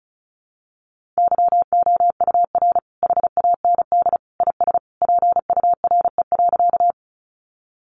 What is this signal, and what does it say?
Morse code, heard as a keyed tone, about 34 wpm, 700 Hz. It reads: YOUR HAND IS PURE.
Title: Sentences from Top 1500 Words 35wpm